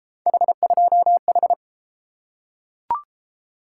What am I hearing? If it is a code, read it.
Morse code, heard as a keyed tone, about 31 wpm, 700 Hz. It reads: H2H